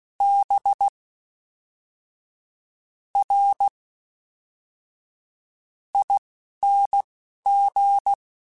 Morse code, heard as a keyed tone, about 16 wpm, spaced as for 12 wpm, 800 Hz. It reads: B R ING
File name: Misc / Misc-mp3s / Jumps @10